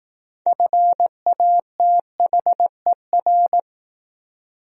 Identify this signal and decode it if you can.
Morse code, heard as a keyed tone, about 18 wpm, 700 Hz. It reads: FATHER
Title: Speed-Racing Top 300 Words 15wpm